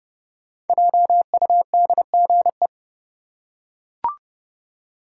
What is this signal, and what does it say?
Morse code, heard as a keyed tone, about 30 wpm, 700 Hz. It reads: JUDGE
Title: Five Letter Words - 30wpm